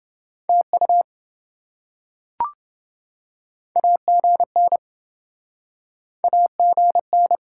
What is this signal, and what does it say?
Morse code, heard as a keyed tone, about 27 wpm, 700 Hz. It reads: TU AGN AGN